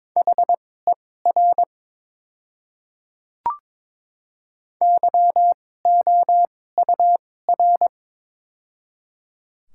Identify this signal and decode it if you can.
Morse code, heard as a keyed tone, about 22 wpm, spaced as for 16 wpm, 700 Hz. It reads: HER YOUR
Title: 2x Character Spacing - Top 100 Words - 22wpm